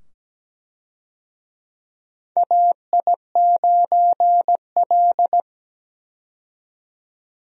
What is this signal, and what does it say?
Morse code, heard as a keyed tone, about 17 wpm, 700 Hz. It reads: AI9L
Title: USA Call Signs - 4 Characters - 17wpm